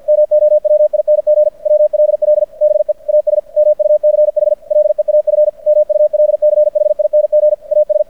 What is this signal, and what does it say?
Morse code, heard as a keyed tone, about 50 wpm, 600 Hz. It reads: MOWING WLL BE AS MUCH LEAF MULCHING AS
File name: archived / CRSnet